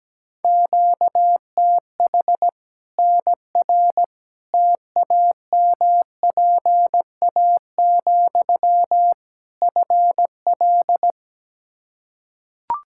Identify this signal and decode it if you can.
Morse code, heard as a keyed tone, about 17 wpm, 700 Hz. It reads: QTH NR TAMPA, FL
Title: Practice QSO - Exchange 1 - Beginning - 17wpm